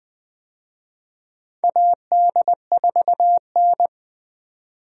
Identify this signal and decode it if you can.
Morse code, heard as a keyed tone, about 20 wpm, 700 Hz. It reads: AD4N
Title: USA Call Signs - 4 Characters - 20wpm